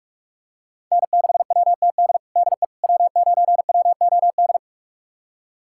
Morse code, heard as a keyed tone, about 46 wpm, 700 Hz. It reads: N6WTD DE W9WOD
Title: USA Callsign DE Callsign - 5 Characters - 45wpm